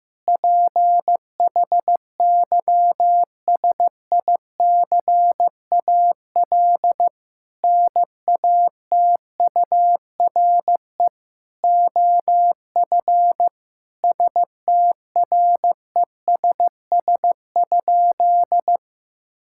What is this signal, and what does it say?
Morse code, heard as a keyed tone, about 15 wpm, 700 Hz. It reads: PHYSICAL NATURE OF STRESS?
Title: Sentences from Top 1000 Words - Review Part 6 - 15wpm